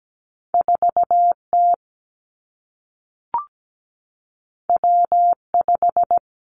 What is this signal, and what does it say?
Morse code, heard as a keyed tone, about 17 wpm, 700 Hz. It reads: 4T W5
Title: Two Challenging Letter-Number Combinations 17wpm